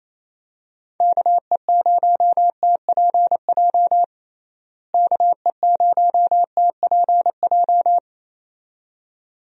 Morse code, heard as a keyed tone, about 28 wpm, 700 Hz. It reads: KE0TPJ KE0TPJ
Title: USA Call Signs - 6 Characters - Repeated 2x - 28wpm